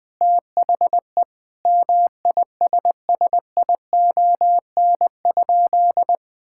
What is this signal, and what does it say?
Morse code, heard as a keyed tone, about 20 wpm, 700 Hz. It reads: THE MISSION?
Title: Sentences from Top 1600 Words 20wpm